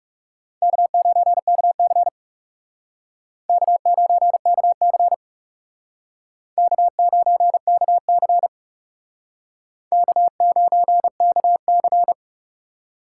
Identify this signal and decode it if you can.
Morse code, heard as a keyed tone, about 37 wpm, 700 Hz. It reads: K9KC K9KC K9KC K9KC
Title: Speed Racing - USA 1x2 Call Signs 30wpm